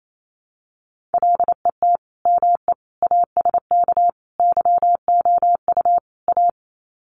Morse code, heard as a keyed tone, about 28 wpm, 700 Hz. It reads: LET ME ASK YOU A
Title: Sets of 5 Words Encore 28wpm